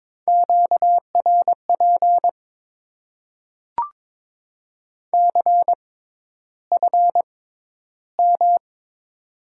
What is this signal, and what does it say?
Morse code, heard as a keyed tone, about 21 wpm, spaced as for 11 wpm, 700 Hz. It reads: QRP CFM
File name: Top 90 QSO Elements - 6x Character Spacing - 22wpm